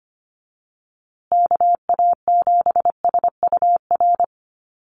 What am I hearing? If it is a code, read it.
Morse code, heard as a keyed tone, about 25 wpm, 700 Hz. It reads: KA7SUR
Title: USA Call Signs Repeated 2x 25wpm